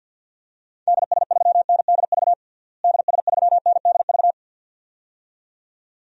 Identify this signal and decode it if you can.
Morse code, heard as a keyed tone, about 50 wpm, 700 Hz. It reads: DS3NDV DS3NDV
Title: DXCC-100 Call Signs - 6 Characters - Repeated 2x - 50wpm